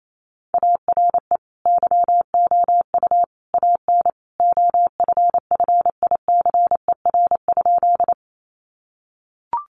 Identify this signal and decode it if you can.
Morse code, heard as a keyed tone, about 28 wpm, 700 Hz. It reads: ARE YOU AN OFFICER?
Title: Sentences from Top 1200 Words 28wpm